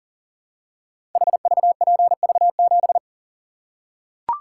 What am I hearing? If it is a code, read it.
Morse code, heard as a keyed tone, about 39 wpm, 700 Hz. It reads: HVPV7
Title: Code Groups - 5 Letter:Numbers - 40wpm